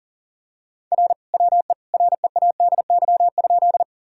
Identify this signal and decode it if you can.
Morse code, heard as a keyed tone, about 40 wpm, 700 Hz. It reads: R WE READY?
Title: Sentences from Top 400 Words and Abbreviations 40wpm